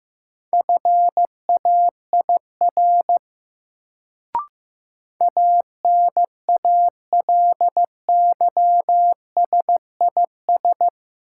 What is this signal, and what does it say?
Morse code, heard as a keyed tone, about 15 wpm, 700 Hz. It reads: FAIR ANALYSIS
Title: Sentences from Top 1000 Words - Review Part 5 - New Word Drill - 15wpm